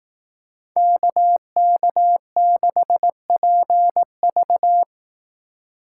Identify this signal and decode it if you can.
Morse code, heard as a keyed tone, about 19 wpm, 700 Hz. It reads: KK6PV
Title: Speed Racing - USA 2x2 Call Signs 15wpm